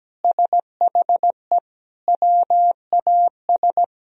Morse code, heard as a keyed tone, about 17 wpm, 700 Hz. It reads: SHE WAS
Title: Sentences from Top 1000 Words - Review Part 1 - 17wpm